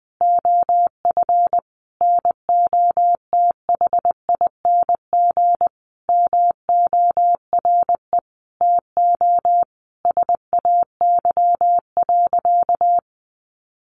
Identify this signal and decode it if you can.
Morse code, heard as a keyed tone, about 20 wpm, 700 Hz. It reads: OF NOTHING MORE TO SAY.
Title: Sentences from Top 500 Words 20wpm